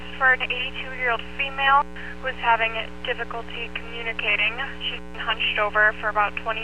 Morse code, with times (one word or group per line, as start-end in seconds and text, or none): none